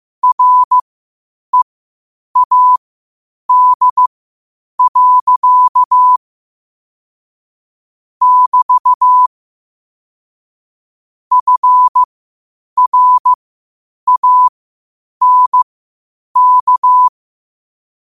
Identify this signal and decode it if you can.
Morse code, heard as a keyed tone, about 15 wpm, spaced as for 8 wpm, 1000 Hz. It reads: READ. <BT> FRANK
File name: 8wpm